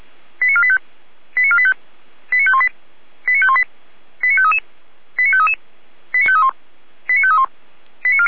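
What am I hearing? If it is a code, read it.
Five-tone selective call (ZVEI), heard as quick runs of five tones, tones of about 70 ms. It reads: 89485 89485 89518 89518 89533 89533 89531 89531
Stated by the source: ZVEI five-tone selective call received on an SDR, logged by the Feuerwehr Orlamünde fire brigade server.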